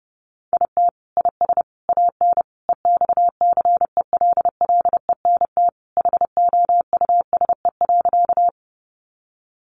Morse code, heard as a keyed tone, about 30 wpm, 700 Hz. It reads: IT IS AN EXCELLENT HOUSE.
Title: Sentences from Top 1000 Words 30wpm